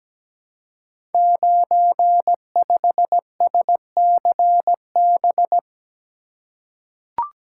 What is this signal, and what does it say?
Morse code, heard as a keyed tone, about 17 wpm, 700 Hz. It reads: 95SCB
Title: Code Groups - 5 Letter:Numbers - 17wpm